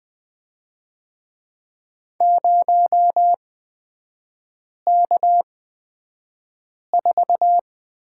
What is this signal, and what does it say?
Morse code, heard as a keyed tone, about 20 wpm, spaced as for 5 wpm, 700 Hz. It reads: 0K4
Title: 250 Lesson 40e - BK - Three Characters 20wpm